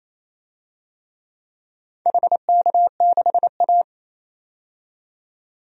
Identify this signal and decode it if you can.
Morse code, heard as a keyed tone, about 28 wpm, 700 Hz. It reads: HK6A
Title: DXCC-100 Call Signs - 4 Characters - 28wpm